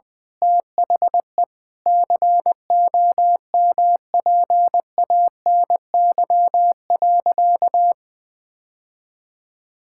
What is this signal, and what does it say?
Morse code, heard as a keyed tone, about 21 wpm, 700 Hz. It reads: THE COMPANY.
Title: Sentences with 8-Word Phrases - Spoken Beginning - 20wpm